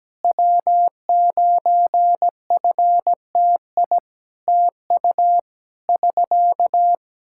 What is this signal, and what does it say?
Morse code, heard as a keyed tone, about 17 wpm, 700 Hz. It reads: W9FTI TU <SK>
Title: Practice QSO - Exchange 4 - Ending - 17wpm